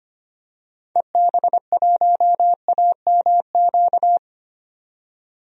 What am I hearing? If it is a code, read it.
Morse code, heard as a keyed tone, about 25 wpm, 700 Hz. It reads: EB1AMQ
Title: DXCC-100 Call Signs - 6 Characters - 25wpm